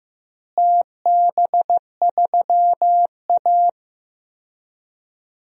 Morse code, heard as a keyed tone, about 15 wpm, 700 Hz. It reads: TB3A
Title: DXCC-100 Call Signs - 4 Characters - Repeated 2x - 15wpm